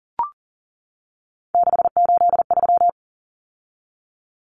Morse code, heard as a keyed tone, about 40 wpm, 700 Hz. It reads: 683
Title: Numbers - 3-Digit 40wpm